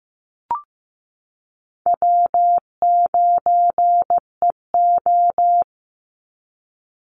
Morse code, heard as a keyed tone, about 15 wpm, 700 Hz. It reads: W9EO
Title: Easy Call Signs Repeated 2x - 15wpm